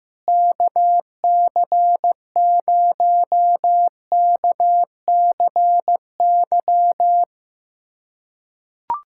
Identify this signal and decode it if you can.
Morse code, heard as a keyed tone, about 15 wpm, 700 Hz. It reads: KC0KCY
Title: USA Callsign DE Callsign 15wpm